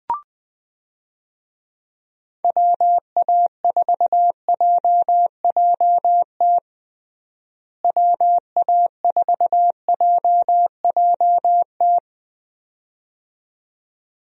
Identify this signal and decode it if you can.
Morse code, heard as a keyed tone, about 20 wpm, 700 Hz. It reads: WA4JJT WA4JJT
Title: USA Call Signs - 6 Characters - Repeated 2x - 20wpm